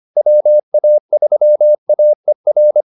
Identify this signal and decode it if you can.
Morse code, heard as a keyed tone, about 25 wpm, 600 Hz. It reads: WA3AER